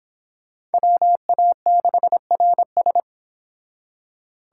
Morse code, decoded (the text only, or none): WA6RS